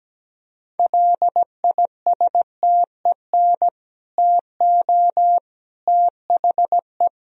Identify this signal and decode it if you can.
Morse code, heard as a keyed tone, about 17 wpm, 700 Hz. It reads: LISTEN TO THE
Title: Sentences from Top 1000 Words - Review Part 3 - 17wpm